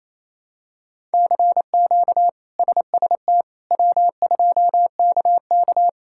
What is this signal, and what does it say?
Morse code, heard as a keyed tone, about 27 wpm, 700 Hz. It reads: CQ SST W2KK
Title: K1USN SST QSO Practice - 28wpm